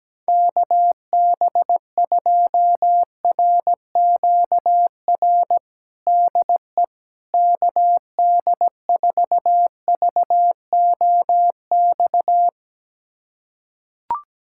Morse code, heard as a keyed tone, about 17 wpm, 700 Hz. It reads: KB2RQR DE KD4VOX
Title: USA Callsign DE Callsign - 6 Characters - 17wpm